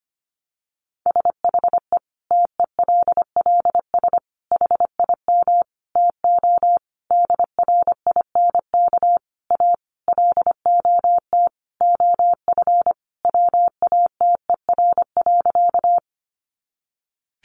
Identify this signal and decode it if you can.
Morse code, heard as a keyed tone, about 25 wpm, 700 Hz. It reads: SHE TELLS HIM TO DRINK A LOT OF WATER.
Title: Additional Easy Phrases 25wpm